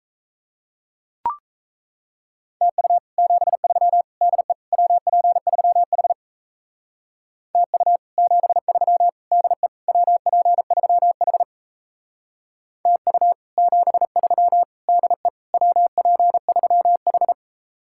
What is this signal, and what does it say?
Morse code, heard as a keyed tone, about 34 wpm, 700 Hz. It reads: TU 73 DE WP3H TU 73 DE WP3H TU 73 DE WP3H
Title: Speed-Racing Common QSO Phrases Encore - 28wpm